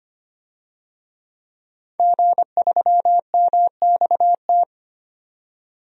Morse code, heard as a keyed tone, about 25 wpm, 700 Hz. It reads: G3MXT
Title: DXCC-100 Call Signs - 5 Characters - 25wpm